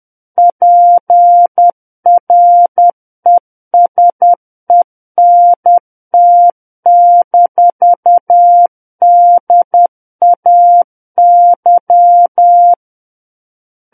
Morse code, heard as a keyed tone, about 10 wpm, 700 Hz. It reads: PRESENT-DAY
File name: Book One / 01-Chapter 1 - 10 WPM